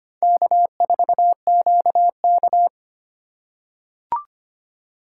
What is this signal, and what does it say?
Morse code, heard as a keyed tone, about 25 wpm, 700 Hz. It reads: K4QK
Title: USA Call Signs - 4 Characters - 4x Character Spacing - 25wpm